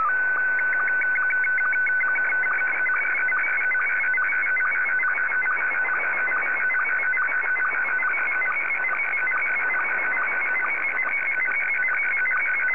SSTV (slow-scan television), Scottie 1.